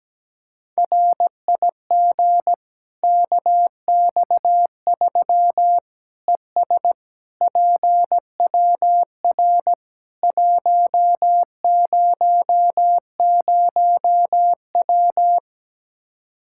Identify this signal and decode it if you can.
Morse code, heard as a keyed tone, about 17 wpm, 700 Hz. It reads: RIG KX3 ES PWR 100W
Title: K1USN SST QSO Practice 17wpm